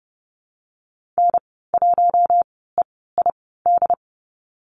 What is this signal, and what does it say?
Morse code, heard as a keyed tone, about 30 wpm, spaced as for 17 wpm, 700 Hz. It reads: N1EID
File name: Lesson 14g - Number 2 - Call Signs 30wpm